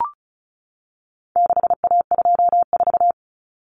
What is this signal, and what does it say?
Morse code, heard as a keyed tone, about 34 wpm, 700 Hz. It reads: 6A24
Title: Common Cut Numbers - 4-Digit 35wpm